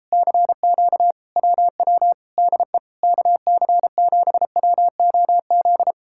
Morse code, heard as a keyed tone, about 38 wpm, 700 Hz. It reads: CQ WW DE KC7WOZ